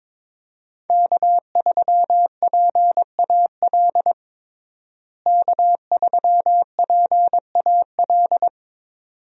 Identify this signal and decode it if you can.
Morse code, heard as a keyed tone, about 22 wpm, 700 Hz. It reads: K3PAL K3PAL
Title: USA Call Signs - 5 Characters - Repeated 2x - 22wpm